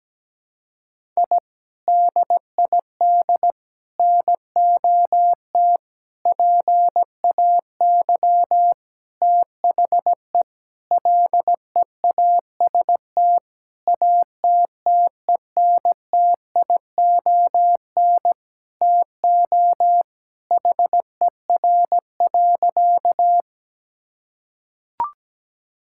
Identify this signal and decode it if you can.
Morse code, heard as a keyed tone, about 17 wpm, 700 Hz. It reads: I DID NOT PAY THE LEAST ATTENTION TO HER.
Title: Sentences from Top 1000 Words - Review Part 6 - 17wpm